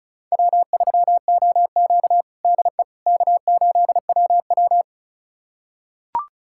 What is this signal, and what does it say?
Morse code, heard as a keyed tone, about 36 wpm, 700 Hz. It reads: W3OQ DE K8WW
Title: USA Callsign DE Callsign - 4 Characters - 35wpm